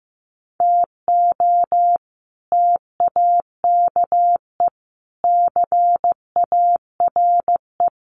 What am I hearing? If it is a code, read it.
Morse code, heard as a keyed tone, about 15 wpm, 700 Hz. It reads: TO TAKE CARE
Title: Sets of 5 Words Encore 15wpm